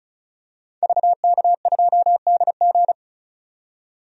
Morse code, heard as a keyed tone, about 36 wpm, 700 Hz. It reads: VK2DG
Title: DXCC-100 Call Signs - 5 Characters - 35wpm